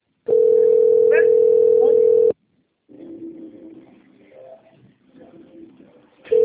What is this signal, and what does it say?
Ringback tone (North America), 440 + 480 Hz, with speech.
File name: rg-1600-unknown- - - .3328